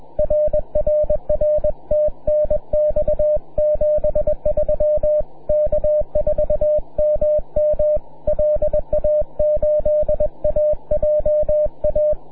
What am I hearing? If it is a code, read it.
Morse code, heard as a keyed tone, about 21 wpm, 600 Hz. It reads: RRR TNX 73 K4MM LA8AJA
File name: wp / k4mm 6m cw